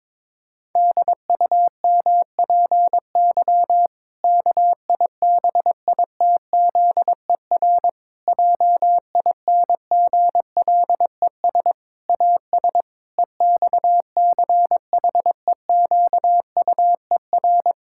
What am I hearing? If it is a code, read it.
Morse code, heard as a keyed tone, about 22 wpm, 700 Hz. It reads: DUMPY KIBITZER JINGLES AS EXCHEQUER
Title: Pangram Challenge v2 - 22wpm